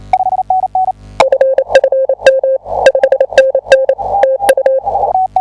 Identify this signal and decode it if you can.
Morse code, heard as a keyed tone, about 27 wpm, 530 Hz. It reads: FFM 5NN TU
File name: tu2t / I1HJT 10m 28 ott